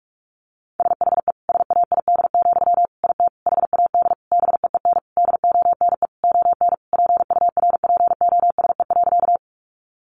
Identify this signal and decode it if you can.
Morse code, heard as a keyed tone, about 44 wpm, 700 Hz. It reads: SHE SAID, IT HAD BEEN DONE ON PURPOSE.
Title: Sentences from Top 800 Words 45wpm